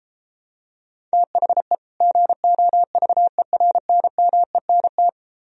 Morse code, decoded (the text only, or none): THE GOVERNMENT